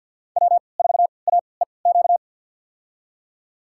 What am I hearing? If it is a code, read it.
Morse code, heard as a keyed tone, about 45 wpm, spaced as for 27 wpm, 700 Hz. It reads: W4AEQ
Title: USA Call Signs - 3x Character Spacing - 50wpm